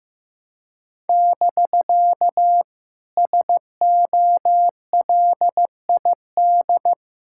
Morse code, heard as a keyed tone, about 15 wpm, 700 Hz. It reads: <BK> SOLID
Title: Speed-Racing Common QSO Phrases Encore - 15wpm